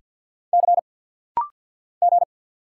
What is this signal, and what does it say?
Morse code, heard as a keyed tone, about 50 wpm, 700 Hz. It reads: C G
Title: Single Letter-Number - Rapid-Fire 50wpm